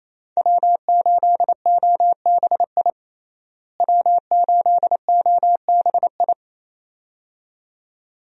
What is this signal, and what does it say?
Morse code, heard as a keyed tone, about 28 wpm, 700 Hz. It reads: W8OBI W8OBI
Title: USA Call Signs - 5 Characters - Repeated 2x - 28wpm